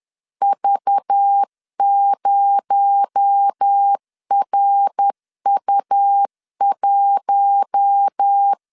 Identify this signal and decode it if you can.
Morse code, heard as a keyed tone, about 10 wpm, 800 Hz. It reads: V0RU1